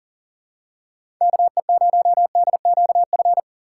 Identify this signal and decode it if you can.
Morse code, heard as a keyed tone, about 40 wpm, 700 Hz. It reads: KE0DQF
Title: USA Call Signs - 6 Characters - 4x Character Spacing - 40wpm